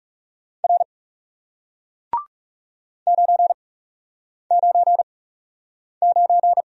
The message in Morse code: R 9 9 9

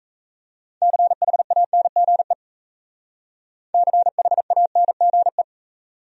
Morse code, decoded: CHANGE CHANGE

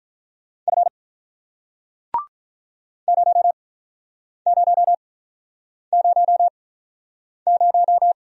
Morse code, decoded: F 0 0 0 0

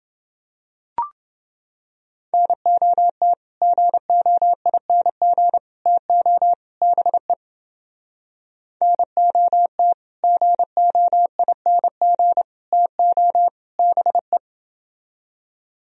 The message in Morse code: NOT GOING TO BE NOT GOING TO BE

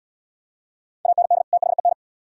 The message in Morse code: AND THEN

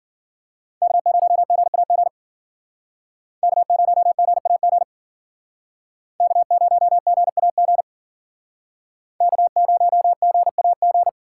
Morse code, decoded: K0GAG K0GAG K0GAG K0GAG

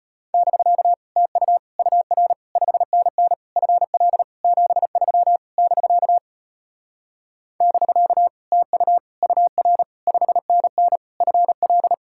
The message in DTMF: none

